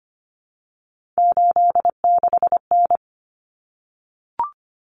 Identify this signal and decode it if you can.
Morse code, heard as a keyed tone, about 25 wpm, 700 Hz. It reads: 86N
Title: Common Cut Numbers - 3-Digit 25wpm